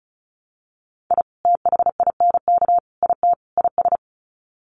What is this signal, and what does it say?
Morse code, heard as a keyed tone, about 34 wpm, 700 Hz. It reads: I THINK IT IS
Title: Sets of 4 Words 35wpm